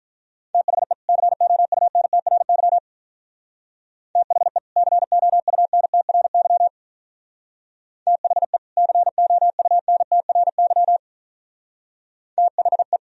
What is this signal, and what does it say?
Morse code, heard as a keyed tone, about 45 wpm, 700 Hz. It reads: THE COUNTRY THE COUNTRY THE COUNTRY THE